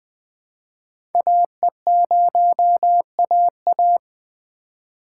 Morse code, heard as a keyed tone, about 20 wpm, 700 Hz. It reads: AE0AA
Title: USA Call Signs - 5 Characters - 4x Character Spacing - 20wpm